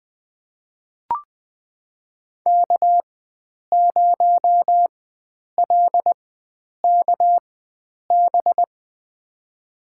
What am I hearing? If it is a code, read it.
Morse code, heard as a keyed tone, about 20 wpm, spaced as for 9 wpm, 700 Hz. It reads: K0LKB
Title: USA Call Signs - 5 Characters - 4x Character Spacing - 20wpm